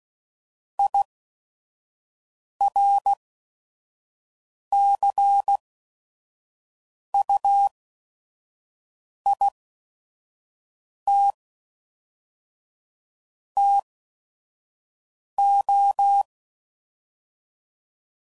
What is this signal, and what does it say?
Morse code, heard as a keyed tone, about 16 wpm, 800 Hz. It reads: I R C U I T T O